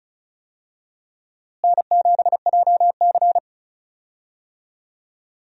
Morse code, heard as a keyed tone, about 36 wpm, 700 Hz. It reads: N7JC
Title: USA Call Signs - 4 Characters - Repeated 2x - 35wpm